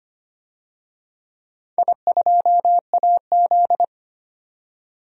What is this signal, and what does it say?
Morse code, heard as a keyed tone, about 25 wpm, 700 Hz. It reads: I2AZ
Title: DXCC-100 Call Signs - 4 Characters - 25wpm